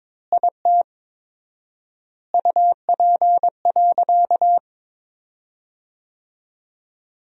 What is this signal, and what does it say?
Morse code, heard as a keyed tone, about 22 wpm, 700 Hz. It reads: IT UP.
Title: 4x Word-Spacing - Sentences from Top 100 Words - 22wpm